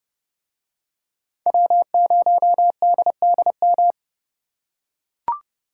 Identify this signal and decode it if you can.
Morse code, heard as a keyed tone, about 29 wpm, 700 Hz. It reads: W0DDM